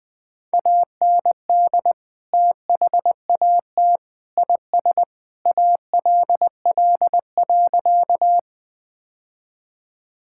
Morse code, heard as a keyed tone, about 20 wpm, 700 Hz. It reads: AND THAT IS ALL.
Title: Sentences with 3-Word Phrases - 20wpm